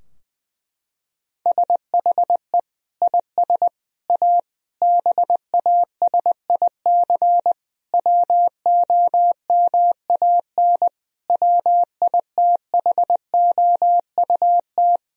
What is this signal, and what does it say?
Morse code, heard as a keyed tone, about 20 wpm, 700 Hz. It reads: SHE IS A BASIC WOMAN WITHOUT
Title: Sentences from Top 1000 Words - Review Part 3 - 20wpm